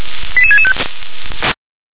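Five-tone selective call (ZVEI), heard as quick runs of five tones, tones of about 70 ms. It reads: 88665
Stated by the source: ZVEI five-tone selective call received via SDR.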